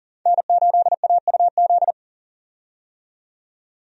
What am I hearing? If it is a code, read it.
Morse code, heard as a keyed tone, about 40 wpm, 700 Hz. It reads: N8AUZ
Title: USA Call Signs - 5 Characters - 40wpm